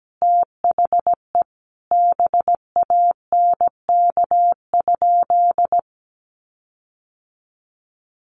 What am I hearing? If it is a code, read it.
Morse code, heard as a keyed tone, about 17 wpm, 700 Hz. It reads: THE BANK?
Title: Sentences from Top 1300 Words 17wpm